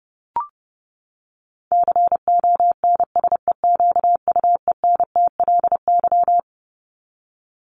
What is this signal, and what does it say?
Morse code, heard as a keyed tone, about 30 wpm, 700 Hz. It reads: CONSEQUENTLY
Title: LY Suffix - Top 100 Words with Base Words 30wpm